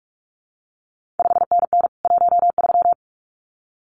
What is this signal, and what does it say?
Morse code, heard as a keyed tone, about 46 wpm, 700 Hz. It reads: 5NN 13
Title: CQ WW DX Contest Exchange - 45wpm